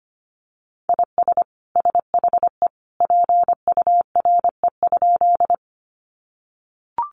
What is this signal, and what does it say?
Morse code, heard as a keyed tone, about 25 wpm, 700 Hz. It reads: IS SHE PURE?
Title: Sentences from Top 1500 Words 25wpm